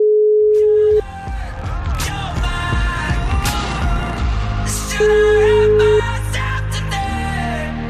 Ringback tone (Europe, 425 Hz), with noise.